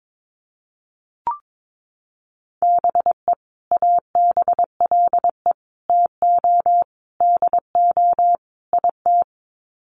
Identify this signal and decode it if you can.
Morse code, heard as a keyed tone, about 22 wpm, 700 Hz. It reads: BE ABLE TO DO IT
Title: Sets of 5 Words Encore2 22wpm